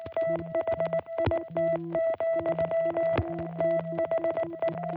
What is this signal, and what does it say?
Morse code, heard as a keyed tone, about 19 wpm, 660 Hz, with noise. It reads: THIT, THE